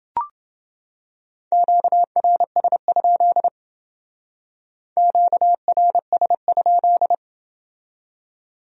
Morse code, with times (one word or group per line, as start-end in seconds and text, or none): 1.5-3.5 QRS?
5.0-7.1 QRS?